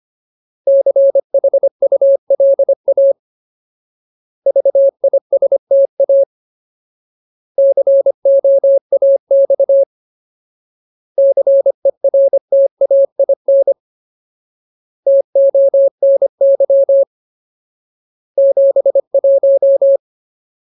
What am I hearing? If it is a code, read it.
Morse code, heard as a keyed tone, about 25 wpm, 550 Hz. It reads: CHULA VISTA COAX CERTAIN TONY 71
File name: Ultimate Word List / 25WPM Ultimate